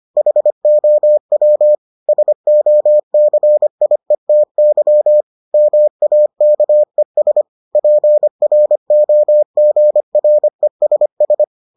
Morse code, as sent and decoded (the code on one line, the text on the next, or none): .... --- .-- / ... --- -.-. .. - -.-- / -- .- -.- . ... / .--. .-. --- --. .-. ... ...
HOW SOCIETY MAKES PROGRESS